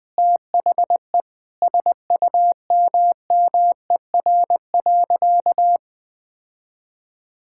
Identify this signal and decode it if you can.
Morse code, heard as a keyed tone, about 21 wpm, 700 Hz. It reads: THE SUMMER.
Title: Sentences with 5-Word Phrases - 20wpm